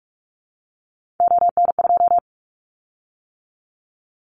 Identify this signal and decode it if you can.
Morse code, heard as a keyed tone, about 44 wpm, 700 Hz. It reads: ON2
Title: Callsign Prefixes for Top 25 DXCC - 45wpm